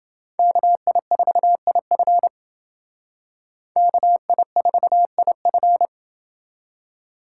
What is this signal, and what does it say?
Morse code, heard as a keyed tone, about 28 wpm, 700 Hz. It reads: KI4IF KI4IF